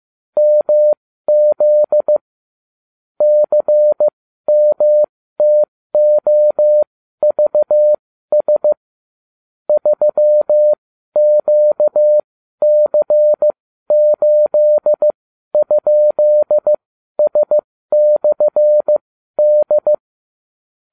Morse code, heard as a keyed tone, about 15 wpm, spaced as for 12 wpm, 600 Hz. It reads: MZ CMTOVS 3QC8?S/D